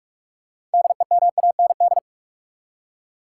Morse code, heard as a keyed tone, about 46 wpm, 700 Hz. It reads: DEMAND
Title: Six Letter Words - 45wpm